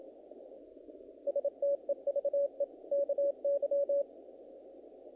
Morse code, heard as a keyed tone, about 27 wpm, 560 Hz. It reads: STEVE KY
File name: contests / naqp-cw-k5tr-op-k5pi- -2000